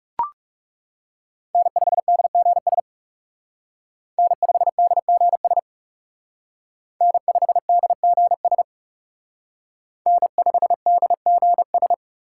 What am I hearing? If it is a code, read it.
Morse code, heard as a keyed tone, about 36 wpm, 700 Hz. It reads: N5DGS N5DGS N5DGS N5DGS
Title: Speed Racing - USA 1x3 Call Signs 30wpm